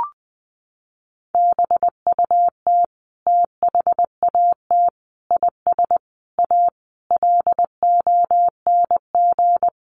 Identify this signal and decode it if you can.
Morse code, heard as a keyed tone, about 20 wpm, 700 Hz. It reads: BUT THAT IS A LONG
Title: Sentences from Top 600 Words 20wpm